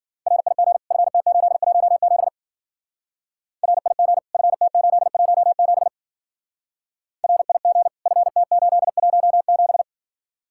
Morse code, heard as a keyed tone, about 48 wpm, 700 Hz. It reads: RIG FT817 RIG FT817 RIG FT817
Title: Speed-Racing Common QSO Phrases Encore - 40wpm